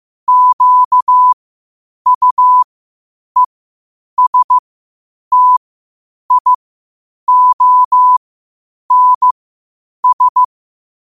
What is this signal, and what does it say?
Morse code, heard as a keyed tone, about 15 wpm, spaced as for 8 wpm, 1000 Hz. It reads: QUESTIONS
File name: qotdfiles / 8wpm